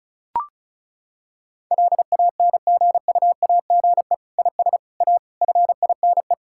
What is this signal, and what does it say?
Morse code, heard as a keyed tone, about 34 wpm, 700 Hz. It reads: LANGUAGE IS A FINE
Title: Sentences from Top 1000 Words - Review Part 6 - 35wpm